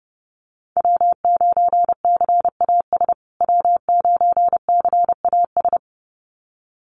Morse code, heard as a keyed tone, about 30 wpm, 700 Hz. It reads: W9CAS W9CAS
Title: USA Call Signs Repeated 2x 30wpm